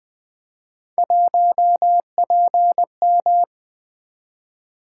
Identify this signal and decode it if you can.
Morse code, heard as a keyed tone, about 20 wpm, 700 Hz. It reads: 1PM
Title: Three Letter-Number Combinations - 20wpm